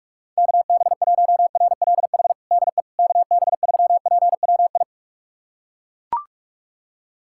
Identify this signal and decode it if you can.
Morse code, heard as a keyed tone, about 41 wpm, 700 Hz. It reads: KB1RLH DE KB3PWI
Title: USA Callsign DE Callsign 45wpm